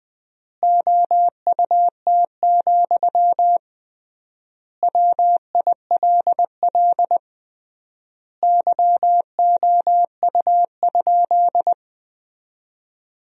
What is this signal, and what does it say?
Morse code, heard as a keyed tone, about 20 wpm, 700 Hz. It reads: OUT, WILL YOU?
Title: 3x Word-Spacing - Sentences from Top 300 Words - 20wpm